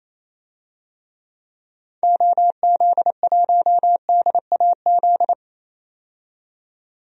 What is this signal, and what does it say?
Morse code, heard as a keyed tone, about 28 wpm, 700 Hz. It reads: OZ1DAZ
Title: DXCC-100 Call Signs - 6 Characters - 28wpm